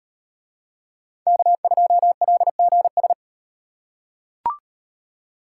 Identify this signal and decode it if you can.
Morse code, heard as a keyed tone, about 36 wpm, 700 Hz. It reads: K2LGS